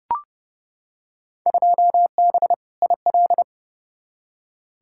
Morse code, heard as a keyed tone, about 31 wpm, 700 Hz. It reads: 2B IL
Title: Field Day Exchanges Encore - 30wpm